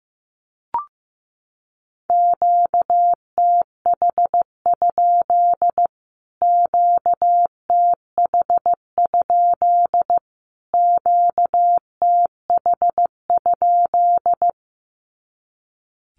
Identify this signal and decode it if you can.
Morse code, heard as a keyed tone, about 15 wpm, 700 Hz. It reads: QTH? QTH? QTH?
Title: Q-Codes Repeated 3x 15wpm